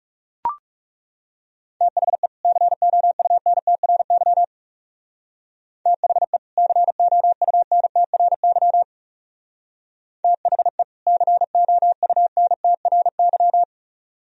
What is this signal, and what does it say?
Morse code, heard as a keyed tone, about 35 wpm, 700 Hz. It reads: THE COUNTRY THE COUNTRY THE COUNTRY
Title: Speed-Racing - Sets of 2 Words 30wpm